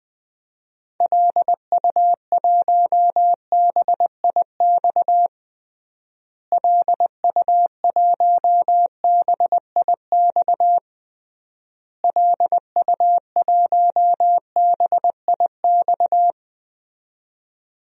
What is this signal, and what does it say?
Morse code, heard as a keyed tone, about 20 wpm, 700 Hz. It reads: LU1BIX LU1BIX LU1BIX
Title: DXCC-100 Call Signs - 6 Characters - Repeated 3x - 20wpm